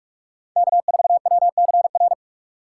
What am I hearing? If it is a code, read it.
Morse code, heard as a keyed tone, about 47 wpm, 700 Hz. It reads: K4WCR
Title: USA Call Signs - 5 Characters - 3x Character Spacing - 45wpm